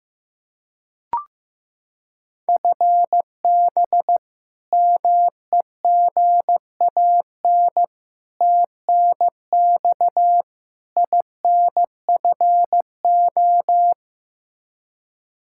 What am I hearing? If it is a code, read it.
Morse code, heard as a keyed tone, about 15 wpm, 700 Hz. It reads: FB MEGAN TNX INFO
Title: Practice QSO - Exchange 3 - More Info - 15wpm